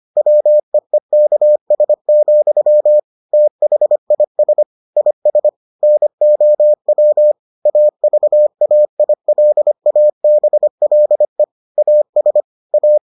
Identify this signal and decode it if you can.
Morse code, heard as a keyed tone, about 25 wpm, 600 Hz. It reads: WEEKS, THIS IS NOW AVAILABLE AS A